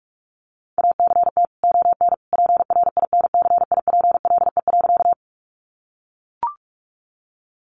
Morse code, heard as a keyed tone, about 44 wpm, 700 Hz. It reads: ACT ON PRINCIPLE.
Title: Sentences from Top 1400 Words 45wpm